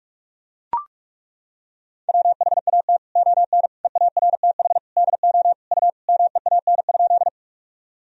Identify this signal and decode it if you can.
Morse code, heard as a keyed tone, about 42 wpm, 700 Hz. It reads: WHAT ON EARTH DO U MEAN?
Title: Sentences from Top 200 Words and Abbreviations 45wpm